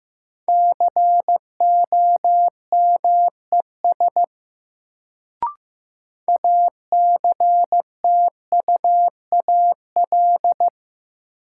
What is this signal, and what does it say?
Morse code, heard as a keyed tone, about 15 wpm, 700 Hz. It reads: COMES ACTUAL
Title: Sentences from Top 1000 Words - Review Part 5 - New Word Drill - 15wpm